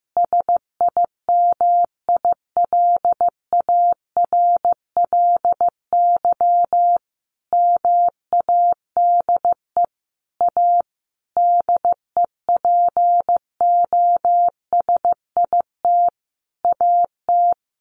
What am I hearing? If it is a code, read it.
Morse code, heard as a keyed tone, about 15 wpm, 700 Hz. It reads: SIMILARLY MADE A DEPOSIT AT
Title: Sentences from Top 1400 Words 15wpm